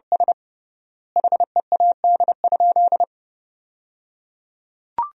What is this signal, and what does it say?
Morse code, heard as a keyed tone, about 30 wpm, 700 Hz. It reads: S HEAD?
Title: 3x Word-Spacing - Sentences from Top 300 Words - 30wpm